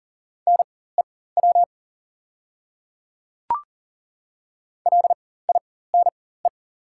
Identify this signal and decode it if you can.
Morse code, heard as a keyed tone, about 38 wpm, spaced as for 18 wpm, 690 Hz. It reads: NEW LINE
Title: 4x Character Spacing - Top 200 Words - 40wpm